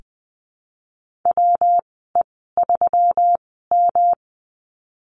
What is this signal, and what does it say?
Morse code, heard as a keyed tone, about 19 wpm, spaced as for 14 wpm, 700 Hz. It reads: WE3M
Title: 122 Lesson 20g - Number 3 - Call Signs 20wpm